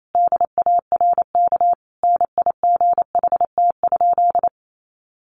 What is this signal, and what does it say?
Morse code, heard as a keyed tone, about 28 wpm, 700 Hz. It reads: DARK NIGHT?
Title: Sentences from Top 1300 Words 28wpm